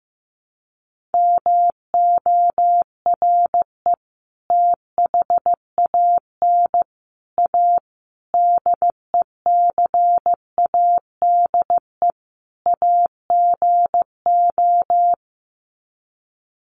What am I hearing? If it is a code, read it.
Morse code, heard as a keyed tone, about 15 wpm, 700 Hz. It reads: MORE THAN A DECADE AGO
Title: Sets of 5 Words Encore2 15wpm